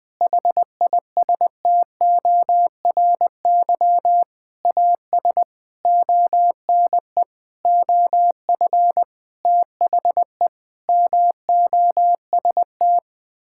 Morse code, heard as a keyed tone, about 20 wpm, 700 Hz. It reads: HISTORY AS ONE OF THE MOST
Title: Sentences with 9-Word Phrases - Spoken Beginning - 20wpm